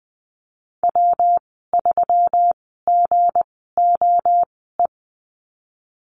Morse code, heard as a keyed tone, about 20 wpm, spaced as for 15 wpm, 700 Hz. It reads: W3GOE
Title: 167 Lesson 27g - Number 7 - Call Signs 20wpm